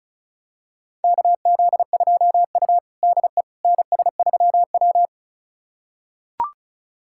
Morse code, heard as a keyed tone, about 36 wpm, 700 Hz. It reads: KZ2U DE NS3W
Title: USA Callsign DE Callsign - 4 Characters - 35wpm